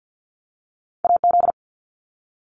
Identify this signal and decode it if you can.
Morse code, heard as a keyed tone, about 50 wpm, 700 Hz. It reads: AZ
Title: ARRL Sections 50wpm